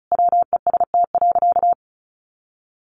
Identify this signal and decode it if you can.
Morse code, heard as a keyed tone, about 34 wpm, 700 Hz. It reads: WEST.
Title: Sentences from Top 1200 Words 35wpm